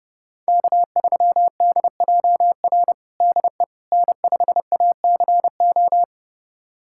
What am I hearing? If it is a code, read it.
Morse code, heard as a keyed tone, about 30 wpm, 700 Hz. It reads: K3DJR DE N5ACO
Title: USA Callsign DE Callsign - 5 Characters - 30wpm